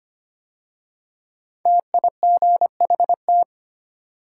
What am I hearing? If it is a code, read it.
Morse code, heard as a keyed tone, about 25 wpm, 700 Hz. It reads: TIGHT
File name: 5 Letter Words - First 3 Letters Given - 25wpm